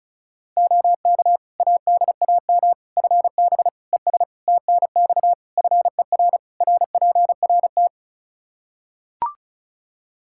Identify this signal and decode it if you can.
Morse code, heard as a keyed tone, about 34 wpm, 700 Hz. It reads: OK ADAM FB ES TNX FER RPRT
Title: Practice QSO - Exchange 2 - Info - 35wpm